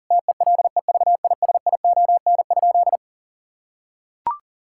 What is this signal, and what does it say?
Morse code, heard as a keyed tone, about 40 wpm, 700 Hz. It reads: TELEVISION?
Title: Sentences from Top 1000 Words - Review Part 5 - 40wpm